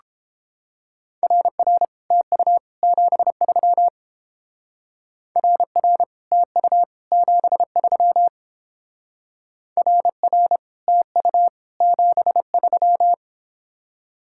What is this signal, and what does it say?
Morse code, heard as a keyed tone, about 31 wpm, 700 Hz. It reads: RR TU 73 RR TU 73 RR TU 73